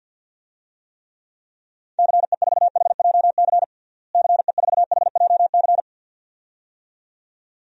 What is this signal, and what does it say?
Morse code, heard as a keyed tone, about 47 wpm, 700 Hz. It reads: CE4HJC CE4HJC
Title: DXCC-100 Call Signs - 6 Characters - Repeated 2x - 50wpm